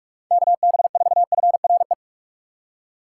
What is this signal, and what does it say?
Morse code, heard as a keyed tone, about 44 wpm, 700 Hz. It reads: KB4FRE
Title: USA Call Signs - 6 Characters - 6x Character Spacing - 45wpm